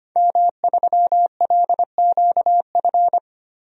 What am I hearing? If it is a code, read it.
Morse code, heard as a keyed tone, about 25 wpm, 700 Hz. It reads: M3LQF
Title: DXCC-100 Call Signs - 5 Characters - Repeated 2x - 25wpm